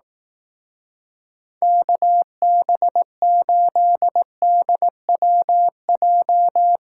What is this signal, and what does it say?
Morse code, heard as a keyed tone, about 18 wpm, 700 Hz. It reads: KB8DWJ
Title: Speed Racing - USA 2x3 Call Signs 15wpm